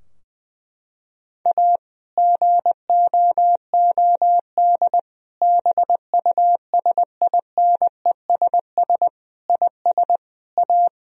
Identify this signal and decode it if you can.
Morse code, heard as a keyed tone, about 20 wpm, 700 Hz. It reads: A GOOD BUSINESS IS A
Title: Sentences from Top 1000 Words - Review Part 4 - 20wpm